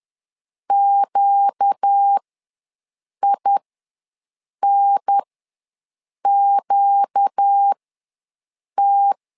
Morse code, heard as a keyed tone, about 10 wpm, spaced as for 6 wpm, 800 Hz. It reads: QINQT
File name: LEZIONI / 4 Q ws 10wpm 2306